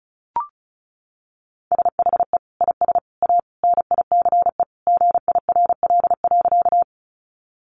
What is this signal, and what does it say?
Morse code, heard as a keyed tone, about 34 wpm, 700 Hz. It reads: SHE IS A NICE GIRL.
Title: Sentences from Top 1100 Words 35wpm